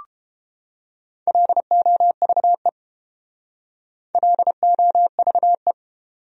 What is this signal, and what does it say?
Morse code, heard as a keyed tone, about 28 wpm, 700 Hz. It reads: LOVE LOVE